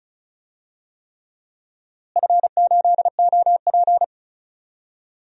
Morse code, heard as a keyed tone, about 36 wpm, 700 Hz. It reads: F8OP
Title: DXCC-100 Call Signs - 4 Characters - Repeated 3x - 35wpm